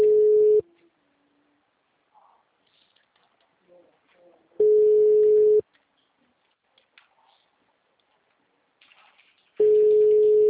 Ringback tone (Europe, 425 Hz).